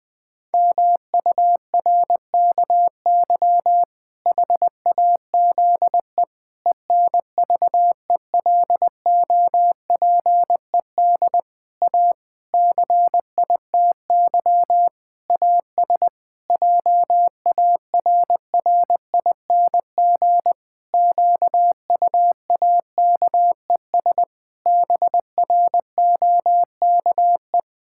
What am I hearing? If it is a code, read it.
Morse code, heard as a keyed tone, about 20 wpm, 700 Hz. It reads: MURKY HAZE ENVELOPED A CITY AS JARRING QUAKES BROKE
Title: Pangram Challenge v2 - 20wpm